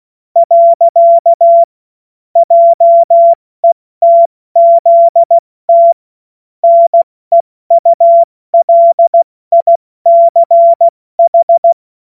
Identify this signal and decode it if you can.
Morse code, heard as a keyed tone, about 16 wpm, 680 Hz. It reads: . JETZT NEULICH
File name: Morsekurs / Witze / Witz 02 April 2024 70 F80